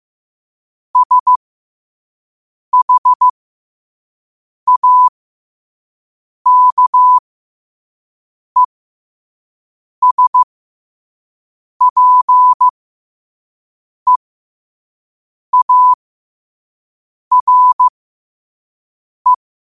Morse code, heard as a keyed tone, about 15 wpm, spaced as for 5 wpm, 1000 Hz. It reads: SHAKESPEARE
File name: 5wpm